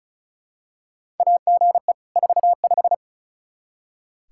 Morse code, heard as a keyed tone, about 36 wpm, 700 Hz. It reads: AGE 45
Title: Practice QSO - Exchange 3 - More Info - 35wpm